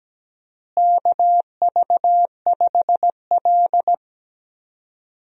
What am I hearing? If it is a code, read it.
Morse code, heard as a keyed tone, about 17 wpm, 700 Hz. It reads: KV5L
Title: USA Call Signs - 4 Characters - Repeated 3x - 17wpm